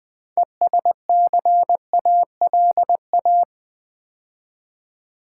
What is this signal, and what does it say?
Morse code, heard as a keyed tone, about 20 wpm, 700 Hz. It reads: ESCALA
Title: Build A Word - 2 Letters At A Time - 8-Letter Words - 20wpm